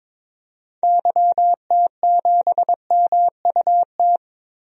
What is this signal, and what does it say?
Morse code, heard as a keyed tone, about 22 wpm, 700 Hz. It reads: YT7MUT
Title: DXCC-100 Call Signs - 6 Characters - Repeated 3x - 22wpm